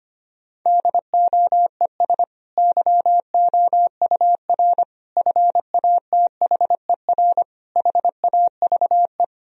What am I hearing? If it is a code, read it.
Morse code, heard as a keyed tone, about 25 wpm, 700 Hz. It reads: DOES YOUR FATHER HAVE
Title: Sentences from Top 1000 Words - Review Part 3 - 25wpm